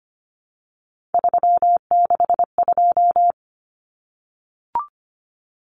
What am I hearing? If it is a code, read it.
Morse code, heard as a keyed tone, about 25 wpm, 700 Hz. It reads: 362
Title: Common Cut Numbers - 3-Digit 25wpm